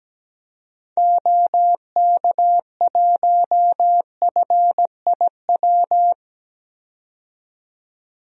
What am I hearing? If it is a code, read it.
Morse code, heard as a keyed tone, about 17 wpm, 700 Hz. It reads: OK1FIW
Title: DXCC-100 Call Signs - 6 Characters - 17wpm